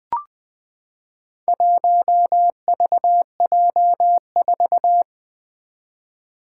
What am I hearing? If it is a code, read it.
Morse code, heard as a keyed tone, about 20 wpm, 700 Hz. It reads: 1VJ4
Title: Four Letter-Number Combinations - 20wpm